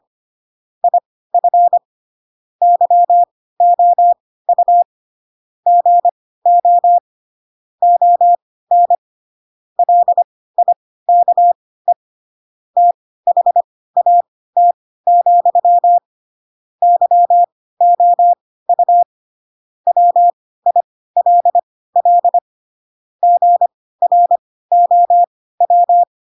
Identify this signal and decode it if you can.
Morse code, heard as a keyed tone, about 25 wpm, spaced as for 16 wpm, 700 Hz. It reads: IF YOU GO ON LIKE THAT, YOU WILL GROW
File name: Pinocchio / Pinocchio ch4 10 25 ik1hge